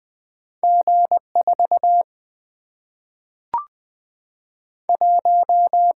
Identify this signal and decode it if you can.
Morse code, heard as a keyed tone, about 20 wpm, 700 Hz. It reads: G 4 1
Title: Single-Letter Single-Number 20wpm